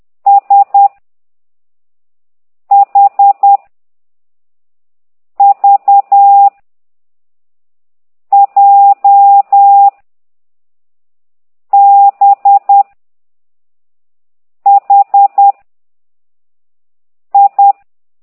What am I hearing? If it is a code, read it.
Morse code, heard as a keyed tone, about 10 wpm, 800 Hz. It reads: S H V J B H I